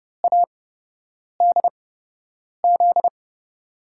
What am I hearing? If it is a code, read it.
Morse code, heard as a keyed tone, about 29 wpm, spaced as for 8 wpm, 700 Hz. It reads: ADZ
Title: USA Call Signs - 5 Characters - 8x Character Spacing - 30wpm